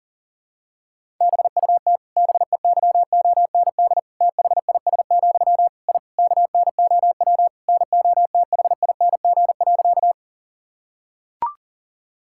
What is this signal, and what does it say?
Morse code, heard as a keyed tone, about 40 wpm, 700 Hz. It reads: BUT BEYOND THIS, I KNOW NOTHING.
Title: Sentences from Top 1000 Words - Review Part 6 - 40wpm